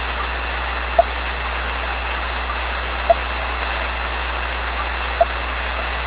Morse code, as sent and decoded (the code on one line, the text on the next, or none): none